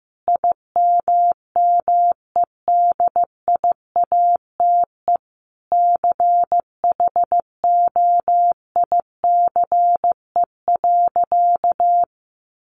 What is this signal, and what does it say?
Morse code, heard as a keyed tone, about 15 wpm, 700 Hz. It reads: IMMEDIATE CHOICE.
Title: Sentences from Top 1200 Words 15wpm